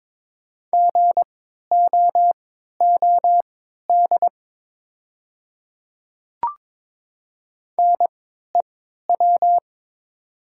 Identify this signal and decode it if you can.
Morse code, heard as a keyed tone, about 22 wpm, spaced as for 12 wpm, 690 Hz. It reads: GOOD NEW
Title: 3x Character Spacing - Top 200 Words - 22wpm